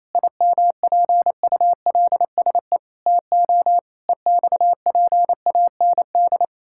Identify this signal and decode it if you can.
Morse code, heard as a keyed tone, about 28 wpm, 700 Hz. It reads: IMPULSE TO EXPAND
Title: Pangram Challenge v2 - 28wpm